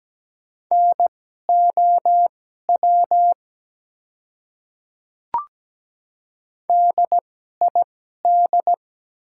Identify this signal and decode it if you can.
Morse code, heard as a keyed tone, about 17 wpm, spaced as for 12 wpm, 700 Hz. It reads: NOW DID
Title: 2x Character Spacing - Top 100 Words - 17wpm